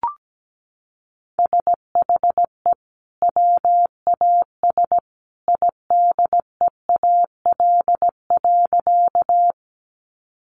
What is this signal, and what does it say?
Morse code, heard as a keyed tone, about 17 wpm, 700 Hz. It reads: SHE WAS IDEAL.
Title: Sentences from Top 1400 Words 17wpm